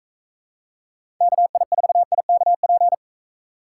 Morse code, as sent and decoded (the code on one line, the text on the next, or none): -.- .. ....- .. -.- .--.
KI4IKP